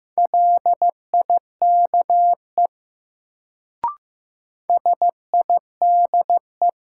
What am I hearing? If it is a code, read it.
Morse code, heard as a keyed tone, about 15 wpm, 700 Hz. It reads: LIKE SIDE
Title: Sets of 2 Words - New Word Drill - 15wpm